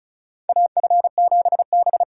Morse code, heard as a keyed tone, about 36 wpm, 700 Hz. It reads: AF7B
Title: USA Call Signs - 4 Characters - 4x Character Spacing - 35wpm